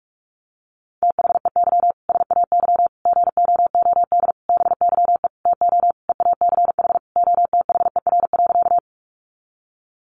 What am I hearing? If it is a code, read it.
Morse code, heard as a keyed tone, about 44 wpm, 700 Hz. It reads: THEY SAY GOOD BYE TO EACH OTHER.
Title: Easy Phrases 45wpm